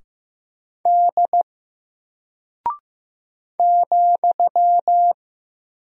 Morse code, heard as a keyed tone, about 15 wpm, 700 Hz. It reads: D ,
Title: Single Character (Letter:Number:Punctuation:Prosign) Alt 15wpm